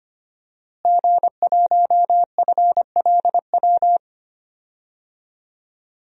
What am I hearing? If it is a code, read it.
Morse code, heard as a keyed tone, about 25 wpm, 700 Hz. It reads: G1FLW
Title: DXCC-100 Call Signs - 5 Characters - Repeated 3x - 25wpm